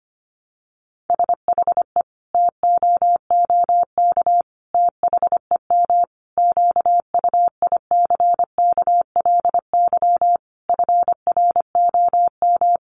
Morse code, heard as a keyed tone, about 25 wpm, 700 Hz. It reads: SHE TOOK THEM QUICKLY FROM
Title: Sentences from Top 1300 Words 25wpm